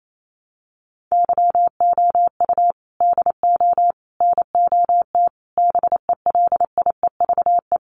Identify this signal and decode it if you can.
Morse code, heard as a keyed tone, about 28 wpm, 700 Hz. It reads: YOU DO NOT BELIEVE
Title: Sentences from Top 1000 Words 28wpm